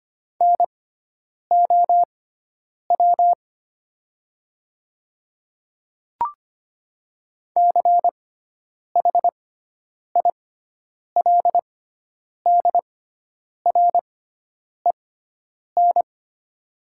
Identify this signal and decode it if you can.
Morse code, heard as a keyed tone, about 25 wpm, spaced as for 9 wpm, 700 Hz. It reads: NOW CHILDREN
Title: 6x Character Spacing - Top 300 Words - 25wpm